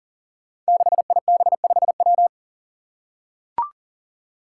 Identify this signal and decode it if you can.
Morse code, heard as a keyed tone, about 40 wpm, 700 Hz. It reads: 6IB5W
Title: Five Challenging Letter-Number Combinations 40wpm